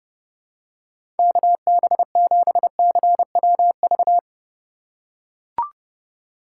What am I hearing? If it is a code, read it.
Morse code, heard as a keyed tone, about 30 wpm, 700 Hz. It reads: KB7CWV
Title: USA Call Signs - 3x Character Spacing - 30wpm